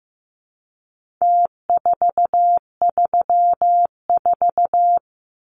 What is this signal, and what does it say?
Morse code, heard as a keyed tone, about 15 wpm, 700 Hz. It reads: T434
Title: Numbers - Contest Style - 4-Digit 15wpm